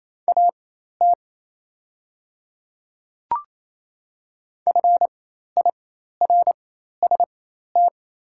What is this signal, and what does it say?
Morse code, heard as a keyed tone, about 28 wpm, spaced as for 13 wpm, 700 Hz. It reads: AT FIRST
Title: 4x Character Spacing - Top 100 Words - 28wpm